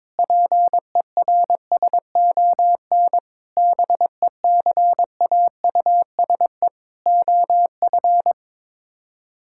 Morse code, heard as a keyed tone, about 22 wpm, 700 Hz. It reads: PERSON BECAUSE OF
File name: Sets of 9 Words - 22wpm